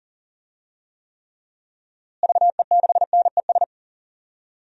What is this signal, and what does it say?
Morse code, heard as a keyed tone, about 40 wpm, 700 Hz. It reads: VE6NES
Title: DXCC-100 Call Signs - 6 Characters - Repeated 3x - 40wpm